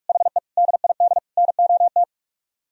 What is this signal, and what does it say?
Morse code, heard as a keyed tone, about 48 wpm, 700 Hz. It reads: HE DID NOT